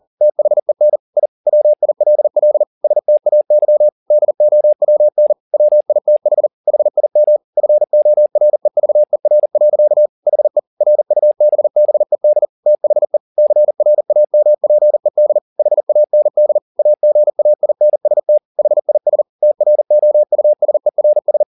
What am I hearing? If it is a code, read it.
Morse code, heard as a keyed tone, about 40 wpm, 600 Hz. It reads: THEN I WILL STAY DOWN WITH HIM FOREVER. HE RUBBED THE CRAMPED HAND AGAINST HIS TROUSERS